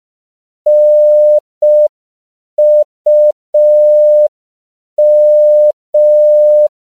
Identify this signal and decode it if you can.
Morse code, heard as a keyed tone, about 5 wpm, 600 Hz. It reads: NUM
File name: CW / 5PPM